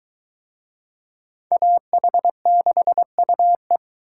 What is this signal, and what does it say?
Morse code, heard as a keyed tone, about 20 wpm, 700 Hz. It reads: AH6UE